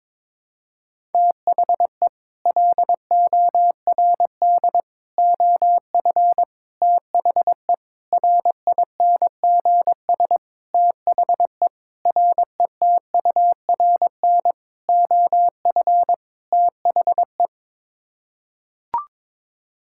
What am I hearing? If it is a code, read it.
Morse code, heard as a keyed tone, about 21 wpm, 700 Hz. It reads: THE LORD OF THE RINGS THE RETURN OF THE
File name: Sets of 9 Words - 22wpm